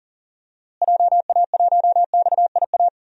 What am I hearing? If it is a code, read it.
Morse code, heard as a keyed tone, about 40 wpm, 700 Hz. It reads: JA1XIA
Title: DXCC-100 Call Signs - 6 Characters - Repeated 2x - 40wpm